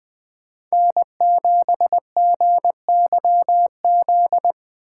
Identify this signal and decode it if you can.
Morse code, heard as a keyed tone, about 20 wpm, 700 Hz. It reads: N7GYZ
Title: Speed Racing - USA 1x3 Call Signs 17wpm